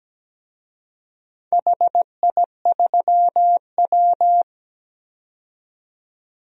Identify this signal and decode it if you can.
Morse code, heard as a keyed tone, about 17 wpm, 700 Hz. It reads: HI3W
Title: DXCC-100 Call Signs - 4 Characters - 17wpm